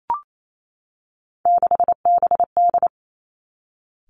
Morse code, heard as a keyed tone, about 28 wpm, 690 Hz. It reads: 6BD
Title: Three Challenging Letter-Number Combinations 28wpm